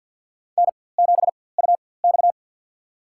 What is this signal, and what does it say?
Morse code, heard as a keyed tone, about 50 wpm, spaced as for 23 wpm, 690 Hz. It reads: N7UX
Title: USA Call Signs - 4x Character Spacing - 50wpm